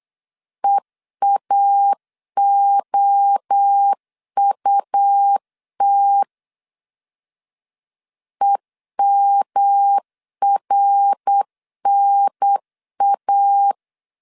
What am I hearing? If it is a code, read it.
Morse code, heard as a keyed tone, about 8 wpm, 800 Hz. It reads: EAOUT EMRNA